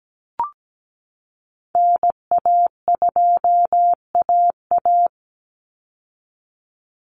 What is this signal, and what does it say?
Morse code, heard as a keyed tone, about 17 wpm, 700 Hz. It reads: NA2AA
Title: Call Signs Repeated 2x - 17wpm